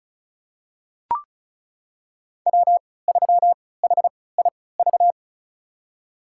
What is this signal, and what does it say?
Morse code, heard as a keyed tone, about 36 wpm, spaced as for 20 wpm, 700 Hz. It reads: W3HIV
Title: USA Call Signs - 5 Characters - 3x Character Spacing - 35wpm